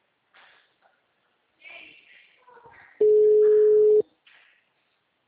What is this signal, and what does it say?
Ringback tone (Europe, 425 Hz).